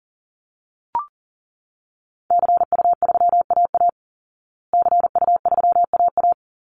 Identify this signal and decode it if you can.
Morse code, heard as a keyed tone, about 40 wpm, 700 Hz. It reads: CU3AA CU3AA
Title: Call Signs Repeated 3x - 40wpm